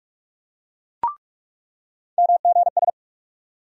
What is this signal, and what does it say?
Morse code, heard as a keyed tone, about 46 wpm, 700 Hz. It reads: MGS
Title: USA Call Sign Postfixes - 3 Letters 45wpm